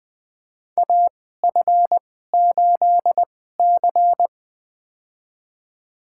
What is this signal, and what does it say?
Morse code, heard as a keyed tone, about 20 wpm, spaced as for 14 wpm, 700 Hz. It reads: AF8C
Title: USA Call Signs - 4 Characters - 2x Character Spacing - 20wpm